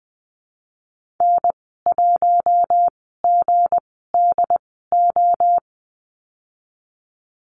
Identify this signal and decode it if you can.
Morse code, heard as a keyed tone, about 19 wpm, spaced as for 14 wpm, 700 Hz. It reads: N1GDO
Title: 167 Lesson 27g - Number 7 - Call Signs 20wpm